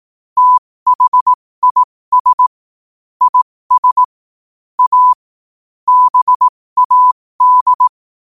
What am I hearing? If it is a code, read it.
Morse code, heard as a keyed tone, about 18 wpm, spaced as for 15 wpm, 1000 Hz. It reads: THIS IS A BAD
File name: qotdfiles / 15wpm